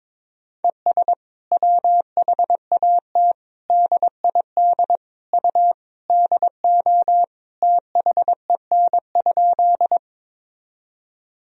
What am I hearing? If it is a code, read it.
Morse code, heard as a keyed tone, about 21 wpm, 700 Hz. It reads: ES WHAT DID U DO THEN?
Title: Sentences from Top 100 Words and Abbreviations 22wpm